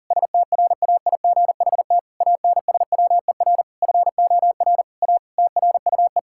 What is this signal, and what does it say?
Morse code, heard as a keyed tone, about 40 wpm, 690 Hz. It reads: STRAIGHT ANSWER FOR A TRUE
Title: Sentences from Top 1000 Words - Review Part 6 - 40wpm